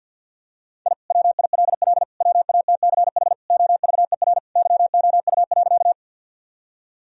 Morse code, heard as a keyed tone, about 51 wpm, 700 Hz. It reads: I WILL WATCH OVER YOU.
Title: Sentences from Top 1000 Words - Review Part 1 - 50wpm